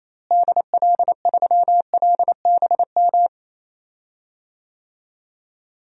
Morse code, heard as a keyed tone, about 28 wpm, 700 Hz. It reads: DL3LBM
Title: DXCC-100 Call Signs - 6 Characters - Repeated 3x - 28wpm